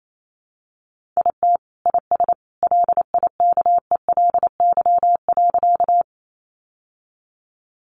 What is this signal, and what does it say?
Morse code, heard as a keyed tone, about 28 wpm, 700 Hz. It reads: IT IS LIKELY.
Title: Sentences from Top 600 Words 28wpm